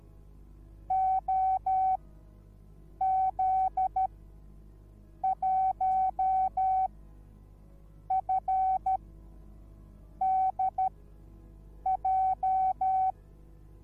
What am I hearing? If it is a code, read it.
Morse code, heard as a keyed tone, about 13 wpm, spaced as for 6 wpm, 760 Hz. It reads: OZ1FDJ